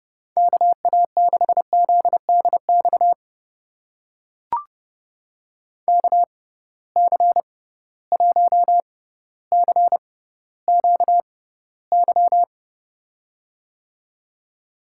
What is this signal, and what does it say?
Morse code, heard as a keyed tone, about 30 wpm, spaced as for 15 wpm, 700 Hz. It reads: KA6ZDX KC1CQY
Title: USA Call Signs - 6 Characters - 6x Character Spacing - 30wpm